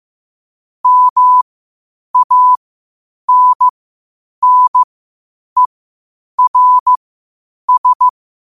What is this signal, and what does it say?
Morse code, heard as a keyed tone, about 15 wpm, spaced as for 8 wpm, 1000 Hz. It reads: MANNERS
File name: qotdfiles / 8wpm